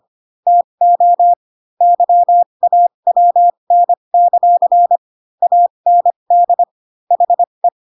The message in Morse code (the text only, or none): TO YAWN; AND HE